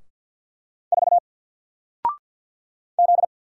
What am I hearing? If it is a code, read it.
Morse code, heard as a keyed tone, about 48 wpm, 700 Hz. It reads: 4 Z
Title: Single Letter-Number 50wpm